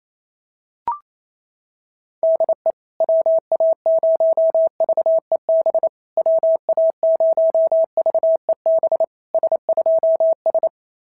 Morse code, heard as a keyed tone, about 29 wpm, 690 Hz. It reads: DE WA0VEB WA0VEB S2S
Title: SOTA QSO Practice 28wpm